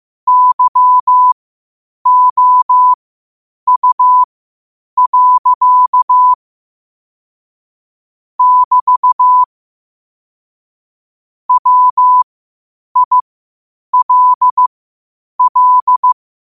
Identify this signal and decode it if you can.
Morse code, heard as a keyed tone, about 15 wpm, spaced as for 8 wpm, 1000 Hz. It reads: YOU. <BT> WILL
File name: qotdfiles / 8wpm